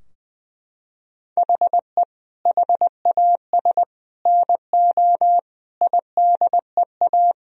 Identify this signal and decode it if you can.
Morse code, heard as a keyed tone, about 20 wpm, 700 Hz. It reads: HE HAS NO IDEA
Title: Sentences from Top 1000 Words - Review Part 3 - 20wpm